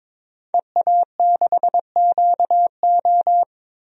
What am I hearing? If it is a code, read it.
Morse code, heard as a keyed tone, about 22 wpm, 700 Hz. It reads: EA6QO
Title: DXCC-100 Call Signs - 5 Characters - 22wpm